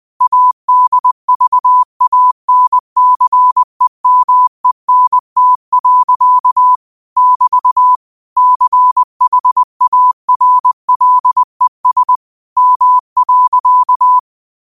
Morse code, heard as a keyed tone, about 20 wpm, 1000 Hz. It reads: ADVANCEMENT. <BT> CHARLES M.